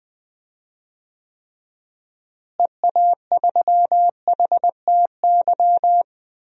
Morse code, heard as a keyed tone, about 21 wpm, 700 Hz. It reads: EA3HTY
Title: DXCC-100 Call Signs - 6 Characters - Repeated 2x - 20wpm